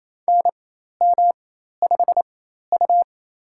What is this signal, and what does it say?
Morse code, heard as a keyed tone, about 28 wpm, spaced as for 13 wpm, 700 Hz. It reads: NM5U
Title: USA Call Signs - 4 Characters - 4x Character Spacing - 28wpm